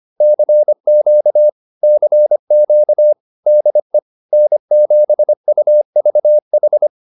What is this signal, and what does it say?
Morse code, heard as a keyed tone, about 25 wpm, 600 Hz. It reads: CQ CQ DE N7UVH
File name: QSO105 25